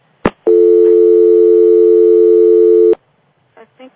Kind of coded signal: telephone call progress tone